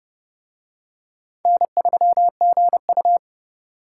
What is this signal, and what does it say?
Morse code, heard as a keyed tone, about 28 wpm, 700 Hz. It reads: N3GU